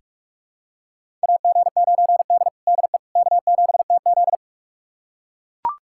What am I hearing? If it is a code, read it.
Morse code, heard as a keyed tone, about 44 wpm, 690 Hz. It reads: AG9D DE K7TZ
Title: USA Callsign DE Callsign - 4 Characters - 45wpm